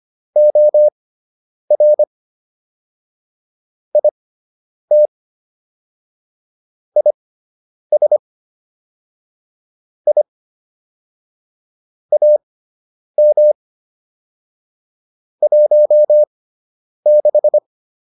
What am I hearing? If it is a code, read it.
Morse code, heard as a keyed tone, about 25 wpm, spaced as for 9 wpm, 600 Hz. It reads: OR IT IS I AM 16